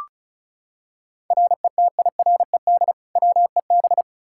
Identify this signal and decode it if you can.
Morse code, heard as a keyed tone, about 36 wpm, 700 Hz. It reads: RETIRED WEB
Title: Practice QSO - Exchange 3 - More Info - 35wpm